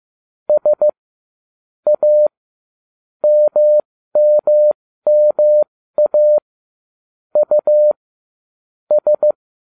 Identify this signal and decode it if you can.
Morse code, heard as a keyed tone, about 15 wpm, spaced as for 13 wpm, 600 Hz. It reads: S A MMMA U S
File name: ubacw05 2